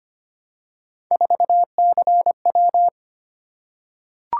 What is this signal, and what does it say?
Morse code, heard as a keyed tone, about 25 wpm, 700 Hz. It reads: 4CW
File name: Three Letter-Number Combinations - 25wpm